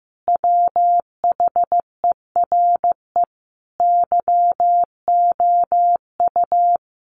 Morse code, heard as a keyed tone, about 15 wpm, 700 Hz. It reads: WHERE YOU
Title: Sentences from Top 200 Words 15wpm